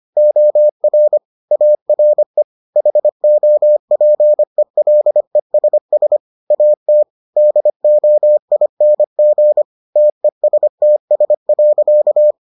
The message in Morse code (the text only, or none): OR ARE HOPELESS AT DOING TESTS.